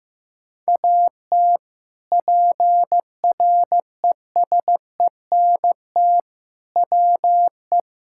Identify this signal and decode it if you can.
Morse code, heard as a keyed tone, about 15 wpm, 700 Hz. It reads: AT PRESENT WE
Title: Sentences from Top 1000 Words - Review Part 5 - 15wpm